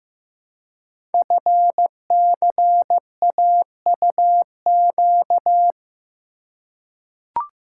Morse code, heard as a keyed tone, about 15 wpm, 690 Hz. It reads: FCAUQ
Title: Code Groups - 5 Letters - 15wpm